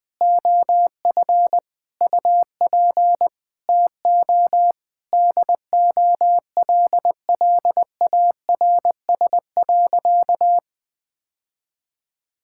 Morse code, heard as a keyed tone, about 21 wpm, 700 Hz. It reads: OF UP TO DOLLARS.
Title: Sentences with 9-Word Phrases - 20wpm